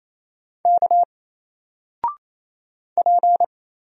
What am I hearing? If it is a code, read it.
Morse code, heard as a keyed tone, about 27 wpm, 700 Hz. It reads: K P
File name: Single-Letter-Number-28wpm